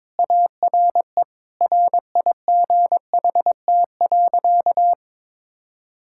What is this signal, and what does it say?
Morse code, heard as a keyed tone, about 22 wpm, 700 Hz. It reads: ARE RIGHT.
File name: practice / 22wpm / Sentences with 3-Word Phrases - 22wpm